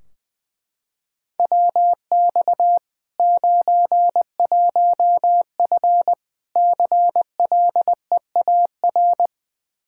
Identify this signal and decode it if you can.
Morse code, heard as a keyed tone, about 20 wpm, 700 Hz. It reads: WX 91F CLEAR
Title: Practice QSO - Exchange 2 - Info - 20wpm